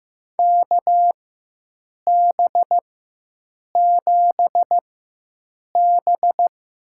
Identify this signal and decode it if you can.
Morse code, heard as a keyed tone, about 15 wpm, spaced as for 7 wpm, 700 Hz. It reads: KB7B
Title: USA Call Signs - 6 Characters - 4x Character Spacing - 15wpm